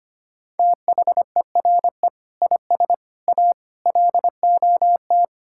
Morse code, heard as a keyed tone, about 25 wpm, 700 Hz. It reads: THERE IS A LOT
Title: Sentences from Top 1000 Words - Review Part 5 - 25wpm